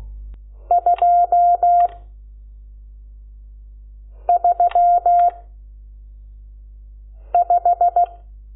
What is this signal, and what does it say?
Morse code, heard as a keyed tone, about 16 wpm, 700 Hz. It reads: 2 3 5
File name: Lecciones / CLASE Nº12